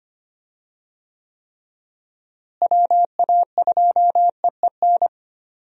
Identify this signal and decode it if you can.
Morse code, heard as a keyed tone, about 25 wpm, 700 Hz. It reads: WA2EEN
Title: USA Call Signs - 6 Characters - Repeated 3x - 25wpm